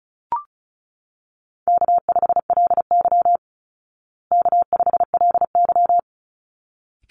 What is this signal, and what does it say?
Morse code, heard as a keyed tone, about 34 wpm, 700 Hz. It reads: K5LY K5LY
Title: Easy Call Signs Repeated 2x - 35wpm